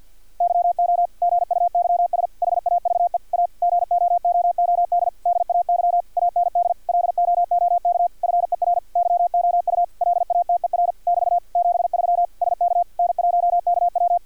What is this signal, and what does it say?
Morse code, heard as a keyed tone, about 51 wpm, 700 Hz, with radio noise. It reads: YO GUYS HAVE A GOOOD DAY AND LOOK FER YOU LATER <BT> 73 SK N1KW
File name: archived / CRSnet